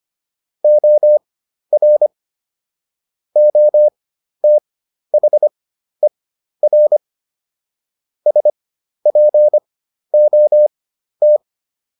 Morse code, heard as a keyed tone, about 25 wpm, spaced as for 12 wpm, 600 Hz. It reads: OR OTHER SPOT